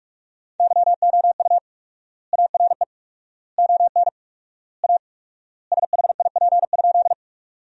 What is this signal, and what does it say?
Morse code, heard as a keyed tone, about 46 wpm, 700 Hz. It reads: YOU ARE ON A SHIP?
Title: 4x Word-Spacing - Sentences from Top 400 Words - 45wpm